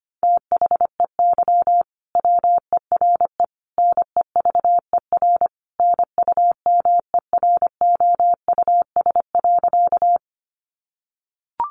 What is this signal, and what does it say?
Morse code, heard as a keyed tone, about 25 wpm, 700 Hz. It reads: THEY WERE NEVER NUMEROUS.
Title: Sentences from Top 1400 Words 25wpm